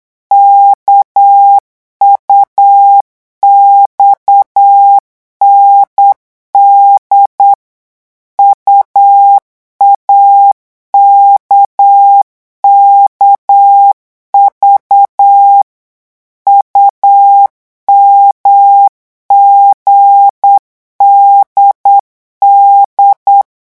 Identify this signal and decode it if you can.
Morse code, heard as a keyed tone, about 8 wpm, 800 Hz. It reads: KUXND UAKKV UMGDD
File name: morse mp3 / revision3-vitesse 8 mots